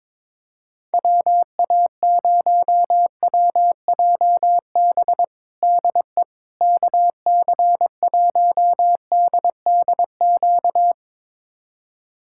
Morse code, heard as a keyed tone, about 22 wpm, 700 Hz. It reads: WA0WJB DE KC1DDQ
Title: USA Callsign DE Callsign - 6 Characters - 22wpm